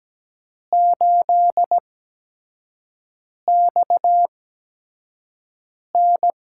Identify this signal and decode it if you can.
Morse code, heard as a keyed tone, about 17 wpm, spaced as for 5 wpm, 700 Hz. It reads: 8XN
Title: USA Call Signs - 4 Characters - 8x Character Spacing - 17wpm